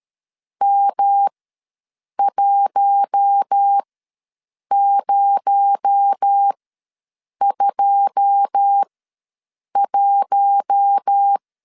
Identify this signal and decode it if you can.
Morse code, heard as a keyed tone, about 13 wpm, spaced as for 7 wpm, 800 Hz. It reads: M1021
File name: LEZIONI / 9 2 ws 12wpm 3406